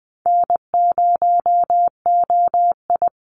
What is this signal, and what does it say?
Morse code, heard as a keyed tone, about 20 wpm, 700 Hz. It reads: N0OI
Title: NA SOTA Activator Callsigns 20wpm